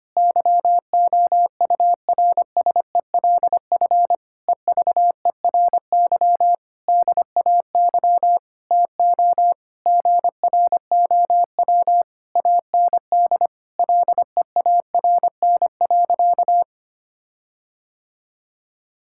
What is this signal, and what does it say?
Morse code, heard as a keyed tone, about 25 wpm, 700 Hz. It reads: YOURSELF EVERY DAY TO GROW AND LEARN.
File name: practice / 25wpm / Difficult Sentences with Hints - 25wpm